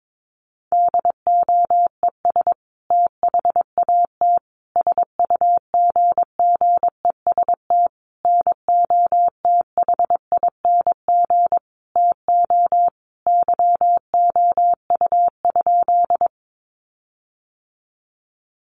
Morse code, heard as a keyed tone, about 22 wpm, 700 Hz. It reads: DOES THAT SUGGEST NOTHING TO YOU?
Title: Sentences from Top 1300 Words 22wpm